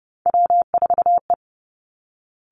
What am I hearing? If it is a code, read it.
Morse code, heard as a keyed tone, about 30 wpm, 700 Hz. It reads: W4E
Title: Three Challenging Letter-Number Combinations 30wpm